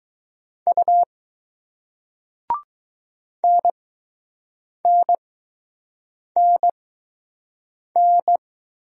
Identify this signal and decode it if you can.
Morse code, heard as a keyed tone, about 20 wpm, spaced as for 6 wpm, 700 Hz. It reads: U NNNN